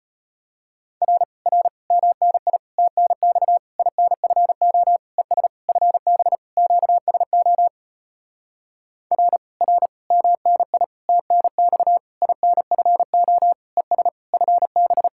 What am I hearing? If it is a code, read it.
Morse code, heard as a keyed tone, about 33 wpm, 700 Hz. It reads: R R MNI TNX INFO ES FB QSO R R MNI TNX INFO ES FB